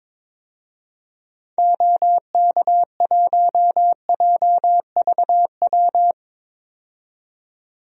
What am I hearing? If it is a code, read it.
Morse code, heard as a keyed tone, about 22 wpm, 700 Hz. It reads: OK1JVW
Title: DXCC-100 Call Signs - 6 Characters - 22wpm